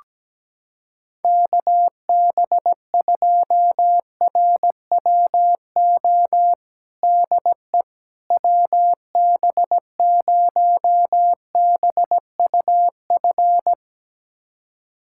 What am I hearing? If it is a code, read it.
Morse code, heard as a keyed tone, about 17 wpm, 700 Hz. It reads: KB2RWO DE WB0BUF
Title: USA Callsign DE Callsign - 6 Characters - 17wpm